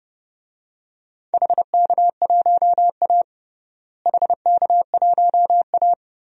Morse code, heard as a keyed tone, about 32 wpm, 700 Hz. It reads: HK1A HK1A
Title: DXCC-100 Call Signs - 4 Characters - Repeated 2x - 30wpm